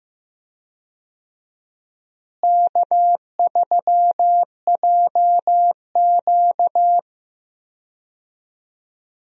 Morse code, heard as a keyed tone, about 15 wpm, 700 Hz. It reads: K3JQ
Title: USA Call Signs - 4 Characters - Repeated 3x - 15wpm